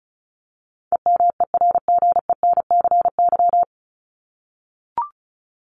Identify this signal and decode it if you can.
Morse code, heard as a keyed tone, about 34 wpm, 700 Hz. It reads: EMERGENCY
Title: Top 1501-1750 Words 35wpm